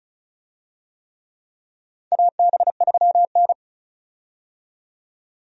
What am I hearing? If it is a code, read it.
Morse code, heard as a keyed tone, about 36 wpm, 700 Hz. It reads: AB3N
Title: USA Call Signs - 4 Characters - 35wpm